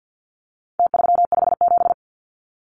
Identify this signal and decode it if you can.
Morse code, heard as a keyed tone, about 50 wpm, 700 Hz. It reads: T357
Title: Numbers - Contest Style - 4-Digit 50wpm